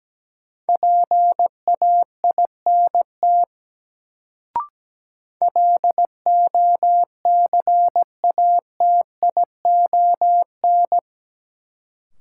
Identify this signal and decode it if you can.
Morse code, heard as a keyed tone, about 17 wpm, 700 Hz. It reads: PAINT LOCATION
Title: Sentences from Top 1000 Words - Review Part 6 - New Word Drill - 17wpm